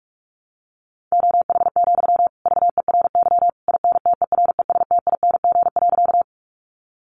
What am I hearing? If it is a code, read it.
Morse code, heard as a keyed tone, about 44 wpm, 700 Hz. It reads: OH, VERY INTERESTING.
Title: Sentences from Top 1000 Words 45wpm